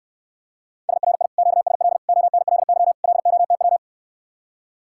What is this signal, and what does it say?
Morse code, heard as a keyed tone, about 69 wpm, 700 Hz. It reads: SRI QSL ONLY LOTW